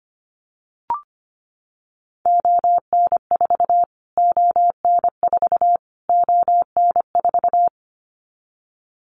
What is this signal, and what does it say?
Morse code, heard as a keyed tone, about 25 wpm, 700 Hz. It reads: ON4 ON4 ON4
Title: Callsign Prefixes for Top 25 DXCC - Repeated 3x - 25wpm